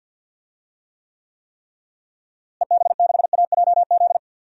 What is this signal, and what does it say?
Morse code, heard as a keyed tone, about 52 wpm, 700 Hz. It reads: EB6AJZ
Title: DXCC-100 Call Signs - 6 Characters - Repeated 3x - 50wpm